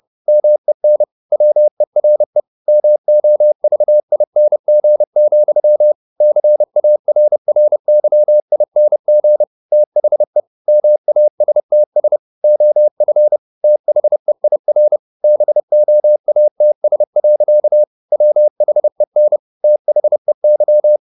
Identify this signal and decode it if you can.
Morse code, heard as a keyed tone, about 30 wpm, 600 Hz. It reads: MEN WERE MOVING, CARRYING THE MASTS OF THEIR BOATS. WHEN THEY
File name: the old man and the sea / the old man and the sea 1 30wpm